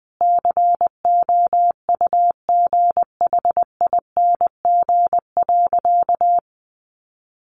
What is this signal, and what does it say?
Morse code, heard as a keyed tone, about 19 wpm, 700 Hz. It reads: COUGHING.
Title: Additional Easy Phrases 20wpm